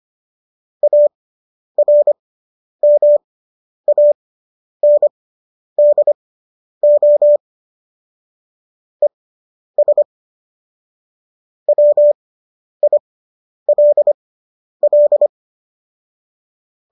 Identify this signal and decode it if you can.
Morse code, heard as a keyed tone, about 25 wpm, spaced as for 10 wpm, 590 Hz. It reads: ARMANDO ES WILL